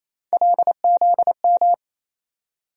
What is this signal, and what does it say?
Morse code, heard as a keyed tone, about 28 wpm, 700 Hz. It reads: LZM
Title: USA Call Sign Postfixes - 3 Letters 28wpm